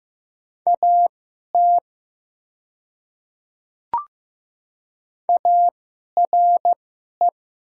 Morse code, heard as a keyed tone, about 15 wpm, spaced as for 11 wpm, 700 Hz. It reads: AT ARE
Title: 2x Character Spacing - Top 100 Words - 15wpm